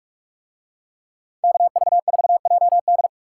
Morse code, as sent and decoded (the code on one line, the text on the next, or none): -.- ...- ....- .--- -..
KV4JD